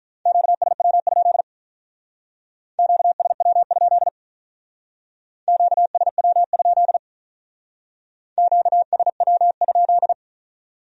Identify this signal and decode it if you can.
Morse code, heard as a keyed tone, about 44 wpm, 700 Hz. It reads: QSW? QSW? QSW? QSW?
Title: Speed-Racing Q-Codes - 35wpm